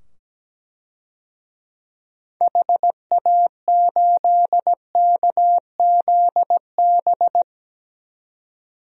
Morse code, heard as a keyed tone, about 17 wpm, 700 Hz. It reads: HA8KZB
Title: DXCC-100 Call Signs - 6 Characters - 17wpm